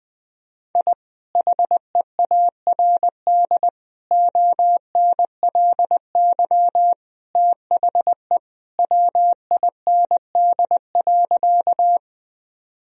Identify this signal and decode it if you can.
Morse code, heard as a keyed tone, about 20 wpm, 700 Hz. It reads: I HEARD ONLY THE WIND.
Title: Sentences from Top 1000 Words - Review Part 2 - 20wpm